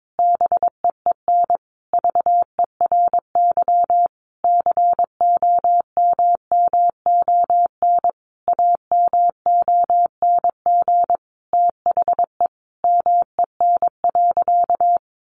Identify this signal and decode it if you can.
Morse code, heard as a keyed tone, about 22 wpm, 700 Hz. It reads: BEEN VERY COMMON AMONG THE MEN.
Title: Sentences from Top 1400 Words 22wpm